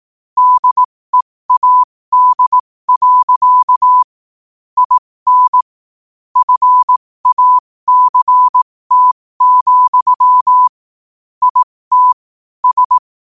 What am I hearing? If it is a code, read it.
Morse code, heard as a keyed tone, about 18 wpm, spaced as for 15 wpm, 1000 Hz. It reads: DEAD. IN FACT, IT S
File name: qotdfiles / 15wpm